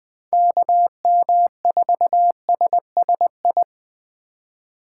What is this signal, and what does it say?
Morse code, heard as a keyed tone, about 20 wpm, 700 Hz. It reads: KM4SSI
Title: USA Call Signs - 6 Characters - 3x Character Spacing - 20wpm